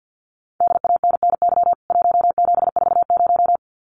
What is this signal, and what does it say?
Morse code, heard as a keyed tone, about 50 wpm, 700 Hz. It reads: DANNY 1740
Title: CWT Exchange 50wpm